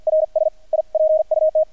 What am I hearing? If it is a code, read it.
Morse code, heard as a keyed tone, about 48 wpm, 630 Hz, with radio noise. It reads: RS I PUT
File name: archived / CRSnet